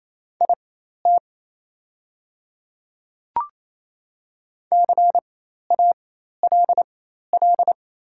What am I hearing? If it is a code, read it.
Morse code, heard as a keyed tone, about 28 wpm, spaced as for 13 wpm, 700 Hz. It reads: IT CALL
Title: 4x Character Spacing - Top 100 Words - 28wpm